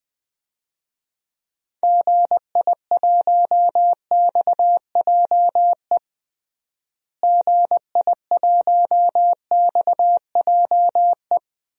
Morse code, heard as a keyed tone, about 21 wpm, 700 Hz. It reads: GI1XJE GI1XJE
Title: DXCC-100 Call Signs - 6 Characters - Repeated 2x - 20wpm